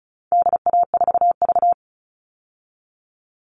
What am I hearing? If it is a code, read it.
Morse code, heard as a keyed tone, about 34 wpm, 700 Hz. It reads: DA4V
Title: All Cut Numbers - 4-Digit 35wpm